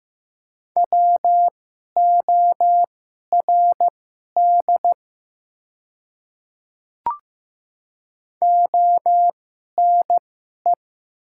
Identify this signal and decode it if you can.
Morse code, heard as a keyed tone, about 15 wpm, spaced as for 11 wpm, 700 Hz. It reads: WORD ONE
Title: 2x Character Spacing - Top 500 Words - 15wpm